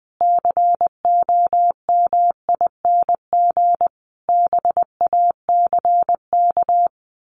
Morse code, heard as a keyed tone, about 20 wpm, 700 Hz. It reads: COMING BACK
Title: A Sound Of Thunder 20wpm